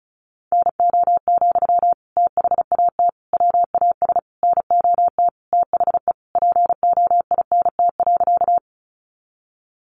Morse code, heard as a keyed tone, about 34 wpm, 700 Hz. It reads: NO, THAT WAS NOT THE POINT.
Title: Sentences from Top 600 Words 35wpm